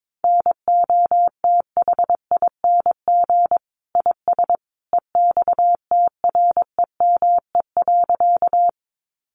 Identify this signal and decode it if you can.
Morse code, heard as a keyed tone, about 22 wpm, 700 Hz. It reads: NOTHING IS EXTREME.
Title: Sentences from Top 1300 Words 22wpm